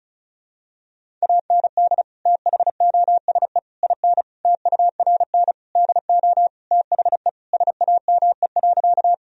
Morse code, heard as a keyed tone, about 36 wpm, 700 Hz. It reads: AND THOSE IN TURN DO THE SAME.
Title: Sentences from Top 1000 Words - Review Part 1 - 35wpm